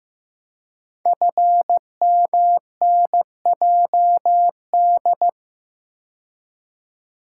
Morse code, heard as a keyed tone, about 15 wpm, 700 Hz. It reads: FMNJD
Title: Code Groups - 5 Letters - 15wpm